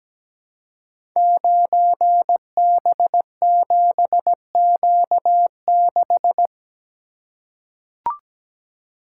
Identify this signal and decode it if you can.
Morse code, heard as a keyed tone, about 17 wpm, 700 Hz. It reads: 9B7Q6
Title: Code Groups - 5 Letter:Numbers - 17wpm